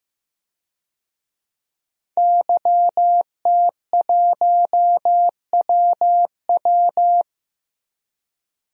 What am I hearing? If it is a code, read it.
Morse code, heard as a keyed tone, about 15 wpm, 700 Hz. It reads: YT1WW
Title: DXCC-100 Call Signs - 5 Characters - 15wpm